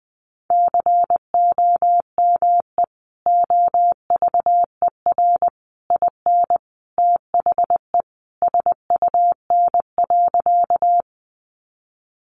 Morse code, heard as a keyed tone, about 20 wpm, 700 Hz. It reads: COME OVER IN THE SUN.
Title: Sentences from Top 300 Words 20wpm